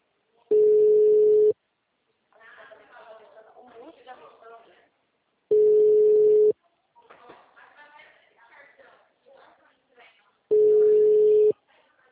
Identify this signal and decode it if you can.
Ringback tone (Europe, 425 Hz), with speech.